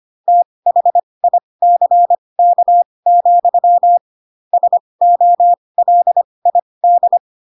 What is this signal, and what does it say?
Morse code, heard as a keyed tone, about 25 wpm, spaced as for 20 wpm, 700 Hz. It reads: THICK, SOLID
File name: Pinocchio / Pinocchio ch1 15 25 ik1hge